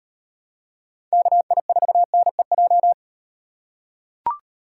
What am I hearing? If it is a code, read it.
Morse code, heard as a keyed tone, about 38 wpm, 700 Hz. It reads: KI4NEJ